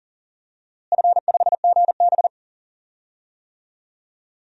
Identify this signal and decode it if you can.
Morse code, heard as a keyed tone, about 37 wpm, 700 Hz. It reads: F5GB
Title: DXCC-100 Call Signs - 4 Characters - Repeated 3x - 40wpm